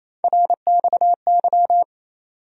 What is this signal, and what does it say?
Morse code, heard as a keyed tone, about 28 wpm, 700 Hz. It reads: RXY
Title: USA Call Sign Postfixes - 3 Letters 28wpm